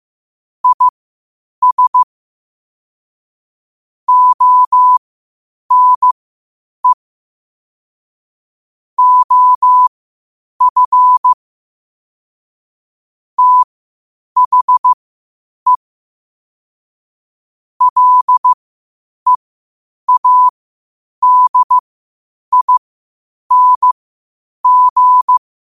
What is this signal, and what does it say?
Morse code, heard as a keyed tone, about 15 wpm, spaced as for 8 wpm, 1000 Hz. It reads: IS ONE OF THE LEADING
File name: qotdfiles / 8wpm